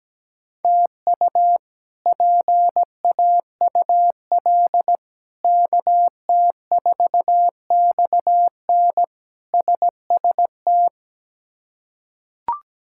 Morse code, heard as a keyed tone, about 17 wpm, 700 Hz. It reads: TU PAUL KT4XN SST
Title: K1USN SST QSO Practice - 17wpm